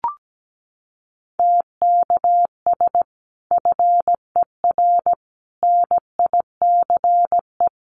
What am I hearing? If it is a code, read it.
Morse code, heard as a keyed tone, about 17 wpm, 700 Hz. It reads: TKS FER NICE
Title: Common QSO Phrases 17wpm